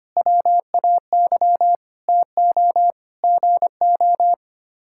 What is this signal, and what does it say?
Morse code, heard as a keyed tone, about 25 wpm, 700 Hz. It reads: WAY TO GO
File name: Sets of 9 Words - 25wpm